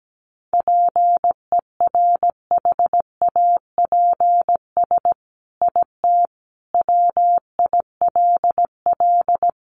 Morse code, heard as a keyed tone, about 17 wpm, 700 Hz. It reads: PERHAPS IT WILL
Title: Sentences from Top 1300 Words 17wpm